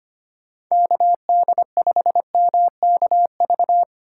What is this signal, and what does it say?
Morse code, heard as a keyed tone, about 25 wpm, 700 Hz. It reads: KD5MKV
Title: USA Call Signs - 2x Character Spacing - 25wpm